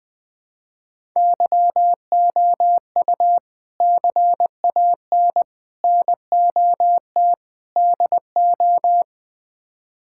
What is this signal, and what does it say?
Morse code, heard as a keyed tone, about 21 wpm, 700 Hz. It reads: YOU CAN NOT DO
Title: Build A Phrase - 5-Word Phrases - 20wpm